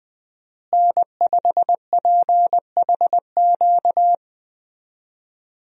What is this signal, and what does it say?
Morse code, heard as a keyed tone, about 21 wpm, 700 Hz. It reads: N5PHQ
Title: USA Call Signs - 5 Characters - Repeated 2x - 20wpm